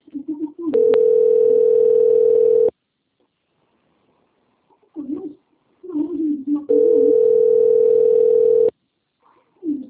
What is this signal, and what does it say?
Ringback tone (North America), 440 + 480 Hz, with speech.
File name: out- - - - - .16651